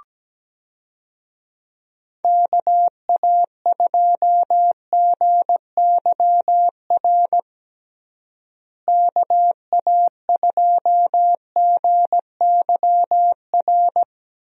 Morse code, heard as a keyed tone, about 17 wpm, 700 Hz. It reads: KA2GYR KA2GYR
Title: USA Call Signs - 6 Characters - Repeated 3x - 17wpm